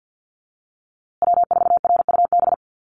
Morse code, heard as a keyed tone, about 50 wpm, 700 Hz. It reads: W4RUB
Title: USA Call Signs Repeated 2x 50wpm